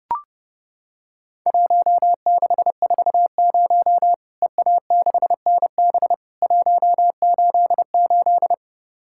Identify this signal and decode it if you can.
Morse code, heard as a keyed tone, about 30 wpm, 700 Hz. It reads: 1640 EA6NB 188
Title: Worked All Europe DX Contest - QTC Exchange - 30wpm